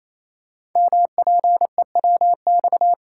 Morse code, heard as a keyed tone, about 29 wpm, 700 Hz. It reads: MPEWX
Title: Code Groups - 5 Letters - 28wpm